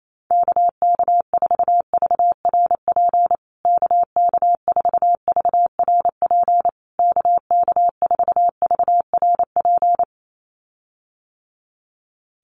Morse code, heard as a keyed tone, about 28 wpm, 700 Hz. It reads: KK4VRP KK4VRP KK4VRP
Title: USA Call Signs Repeated 3x 28wpm